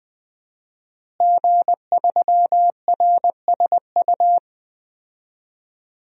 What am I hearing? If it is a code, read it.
Morse code, heard as a keyed tone, about 20 wpm, 700 Hz. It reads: G3RSU
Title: DXCC-100 Call Signs - 5 Characters - Repeated 2x - 20wpm